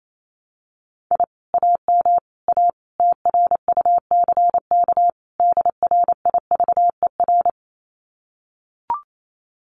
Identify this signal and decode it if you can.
Morse code, heard as a keyed tone, about 28 wpm, 700 Hz. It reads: I AM A TRUCK DRIVER
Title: Common QSO Phrases Encore 28wpm